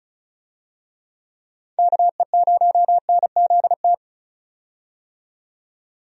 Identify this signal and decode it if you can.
Morse code, heard as a keyed tone, about 34 wpm, 700 Hz. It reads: KE0NZT
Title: USA Call Signs - 6 Characters - 35wpm